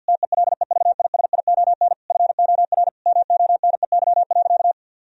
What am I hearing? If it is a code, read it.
Morse code, heard as a keyed tone, about 50 wpm, 700 Hz. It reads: TELEVISION FOR MONEY.
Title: Sentences from Top 1000 Words - Review Part 4 - 50wpm